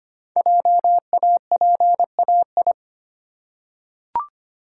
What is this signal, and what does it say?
Morse code, heard as a keyed tone, about 25 wpm, 700 Hz. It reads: JAPAI
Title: Code Groups - 5 Letter:Numbers - 25wpm